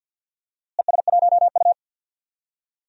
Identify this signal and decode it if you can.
Morse code, heard as a keyed tone, about 47 wpm, 700 Hz. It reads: ES1U
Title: DXCC-100 Call Signs - 4 Characters - Repeated 2x - 50wpm